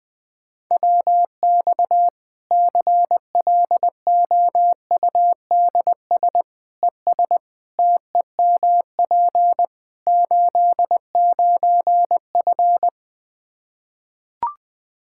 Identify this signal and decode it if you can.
Morse code, heard as a keyed tone, about 21 wpm, 700 Hz. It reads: WX CLOUDS ES TEMP 89F
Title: Practice QSO - Exchange 2 - Info - 20wpm